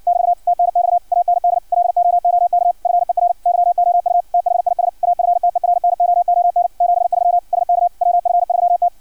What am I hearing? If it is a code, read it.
Morse code, heard as a keyed tone, about 56 wpm, 700 Hz. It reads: <BT> TNX AND LOOK FER YOU THIS AFTERNOON 73 SK WF3T
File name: archived / CRSnet